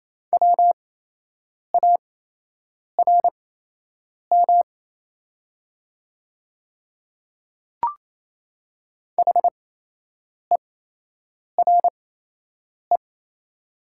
Morse code, heard as a keyed tone, about 27 wpm, spaced as for 8 wpm, 700 Hz. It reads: WARM HERE
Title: 8x Character Spacing - Top 500 Words - 28wpm